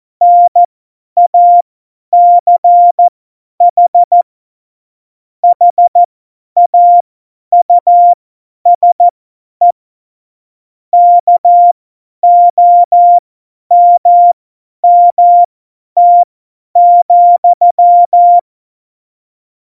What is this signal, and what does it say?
Morse code, heard as a keyed tone, about 14 wpm, spaced as for 10 wpm, 700 Hz. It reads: NACH HAUSE KOMMT,
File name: Morsekurs / Witze / Witz 18 August 2020, 50 F70